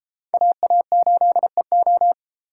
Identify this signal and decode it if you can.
Morse code, heard as a keyed tone, about 33 wpm, 700 Hz. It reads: AA8EO